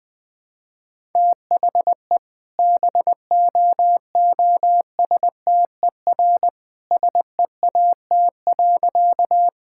Morse code, heard as a keyed tone, about 21 wpm, 700 Hz. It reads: THE BOOSTER SEAT.
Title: All-Age Appropriate Jokes - 20wpm